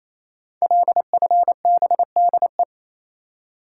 Morse code, heard as a keyed tone, about 28 wpm, 700 Hz. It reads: LFBDE
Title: Code Groups - 5 Letters - 28wpm